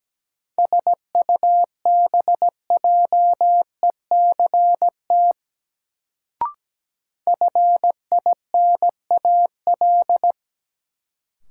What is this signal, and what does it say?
Morse code, heard as a keyed tone, about 17 wpm, 700 Hz. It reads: SUBJECT FINAL
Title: Sentences from Top 800 Words - New Word Drill - 17wpm